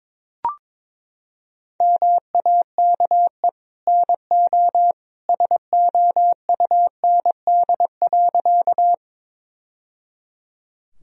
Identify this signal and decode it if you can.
Morse code, heard as a keyed tone, about 22 wpm, 700 Hz. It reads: MAKE NO SOUND.
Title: Sentences from Top 1000 Words - Review Part 1 - 22wpm